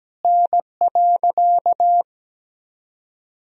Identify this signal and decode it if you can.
Morse code, heard as a keyed tone, about 17 wpm, 700 Hz. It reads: N.
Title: Sentences from Top 300 Words and Abbreviations 17wpm